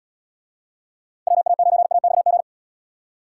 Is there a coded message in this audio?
Yes: Morse code from 1.3 to 2.4 s, heard as a keyed tone, about 78 wpm, 700 Hz. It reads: WA9AXG